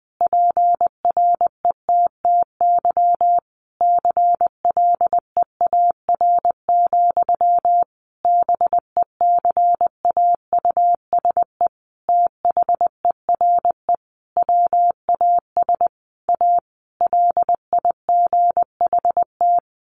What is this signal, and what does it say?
Morse code, heard as a keyed tone, about 20 wpm, 700 Hz. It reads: PRETTY CLEAR, BECAUSE THERE WAS A LIGHT
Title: Sentences from Top 1000 Words 20wpm